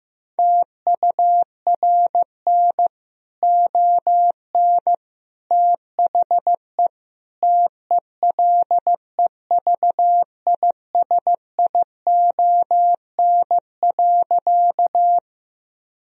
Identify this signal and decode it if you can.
Morse code, heard as a keyed tone, about 15 wpm, 700 Hz. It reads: TURN ON THE TELEVISION.
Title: Sentences from Top 1000 Words - Review Part 4 - 15wpm